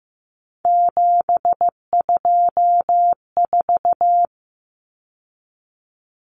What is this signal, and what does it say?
Morse code, heard as a keyed tone, about 15 wpm, 700 Hz. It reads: 724
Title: Numbers - 3-Digit 15wpm